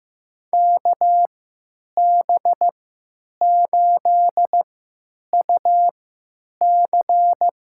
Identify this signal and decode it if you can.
Morse code, heard as a keyed tone, about 15 wpm, spaced as for 9 wpm, 700 Hz. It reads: KB8UC
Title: USA Call Signs - 6 Characters - 3x Character Spacing - 15wpm